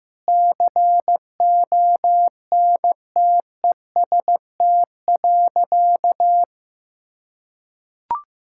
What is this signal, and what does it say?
Morse code, heard as a keyed tone, about 15 wpm, 700 Hz. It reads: CONTEST.
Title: Sentences from Top 1600 Words 15wpm